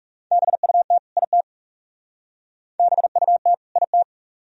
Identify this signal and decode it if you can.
Morse code, heard as a keyed tone, about 37 wpm, 700 Hz. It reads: BUT IT BUT IT
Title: Speed-Racing - Sets of 2 Words 30wpm